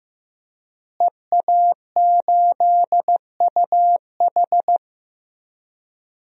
Morse code, heard as a keyed tone, about 15 wpm, 700 Hz. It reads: EA8UH
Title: DXCC-100 Call Signs - 5 Characters - Repeated 2x - 15wpm